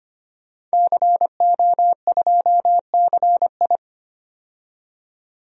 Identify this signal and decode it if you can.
Morse code, heard as a keyed tone, about 25 wpm, 700 Hz. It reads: CO2CI
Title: DXCC-100 Call Signs - 5 Characters - 25wpm